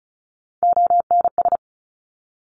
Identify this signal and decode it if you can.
Morse code, heard as a keyed tone, about 36 wpm, 700 Hz. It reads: ONS
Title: Top Three Letter Combinations in Words 35wpm